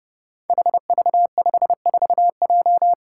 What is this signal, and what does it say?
Morse code, heard as a keyed tone, about 30 wpm, 700 Hz. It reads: HV54J
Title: Five Challenging Letter-Number Combinations 30wpm